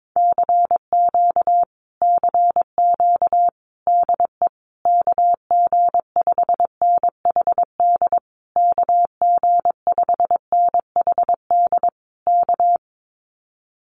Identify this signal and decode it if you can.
Morse code, heard as a keyed tone, about 22 wpm, 700 Hz. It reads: CQ CQ DE KG5NHD KG5NHD K
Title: Common QSO Phrases Encore 22wpm